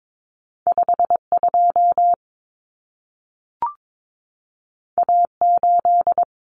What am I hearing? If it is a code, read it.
Morse code, heard as a keyed tone, about 22 wpm, 700 Hz. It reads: 52 A8
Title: Common Cut Numbers - 2-Digit 22wpm